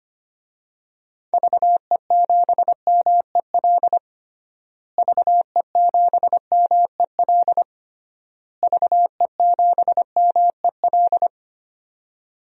Morse code, heard as a keyed tone, about 25 wpm, 700 Hz. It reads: VE7MEL VE7MEL VE7MEL
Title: DXCC-100 Call Signs - 6 Characters - Repeated 3x - 25wpm